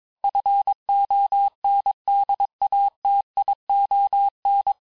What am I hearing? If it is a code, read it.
Morse code, heard as a keyed tone, about 22 wpm, 800 Hz. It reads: FONDATION
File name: texte clair vitesse 16 puis 21 mpm